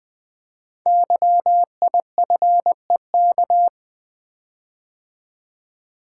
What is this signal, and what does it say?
Morse code, heard as a keyed tone, about 20 wpm, 700 Hz. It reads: YIFEK
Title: Code Groups - 5 Letters - 20wpm